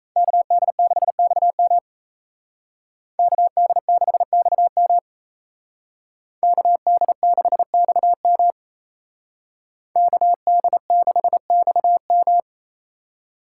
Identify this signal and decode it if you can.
Morse code, heard as a keyed tone, about 36 wpm, 700 Hz. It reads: KD6XM KD6XM KD6XM KD6XM